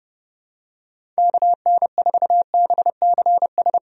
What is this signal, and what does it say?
Morse code, heard as a keyed tone, about 32 wpm, 700 Hz. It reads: KN4BCS
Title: USA Call Signs - 6 Characters - Repeated 2x - 30wpm